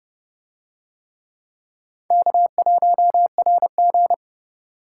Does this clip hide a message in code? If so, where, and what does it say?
Yes: Morse code from 2.1 to 4.1 s, heard as a keyed tone, about 28 wpm, 700 Hz. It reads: K1RG